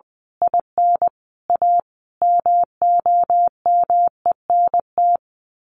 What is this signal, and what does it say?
Morse code, heard as a keyed tone, about 20 wpm, 700 Hz. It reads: IN A MOMENT
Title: Sets of 5 Words 20wpm